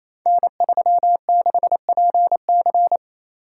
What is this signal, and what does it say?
Morse code, heard as a keyed tone, about 29 wpm, 700 Hz. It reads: N36PC
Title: Code Groups - 5 Letter:Numbers - 28wpm